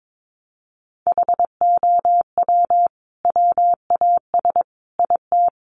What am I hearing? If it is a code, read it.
Morse code, heard as a keyed tone, about 22 wpm, 700 Hz. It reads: HOW WAS IT
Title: Sentences from Top 500 Words 22wpm